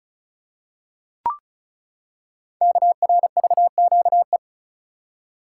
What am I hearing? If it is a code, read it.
Morse code, heard as a keyed tone, about 36 wpm, 700 Hz. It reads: KRVQE
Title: Code Groups - 5 Letters - 35wpm